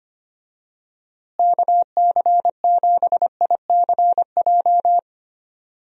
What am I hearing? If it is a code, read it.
Morse code, heard as a keyed tone, about 25 wpm, 700 Hz. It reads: KC7ICJ
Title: USA Call Signs - 6 Characters - Repeated 3x - 25wpm